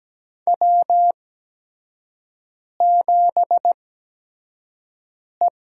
Morse code, heard as a keyed tone, about 17 wpm, spaced as for 5 wpm, 700 Hz. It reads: W7E
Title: USA Call Signs - 8x Character Spacing - 17wpm